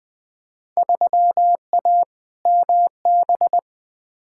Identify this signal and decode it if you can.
Morse code, heard as a keyed tone, about 20 wpm, 700 Hz. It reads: 3A MB
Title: Field Day Exchanges Encore - 20wpm